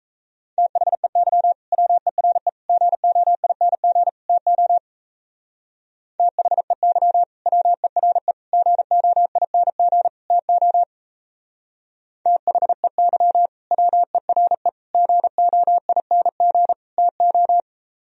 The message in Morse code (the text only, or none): THEY WERE GOING TO THEY WERE GOING TO THEY WERE GOING TO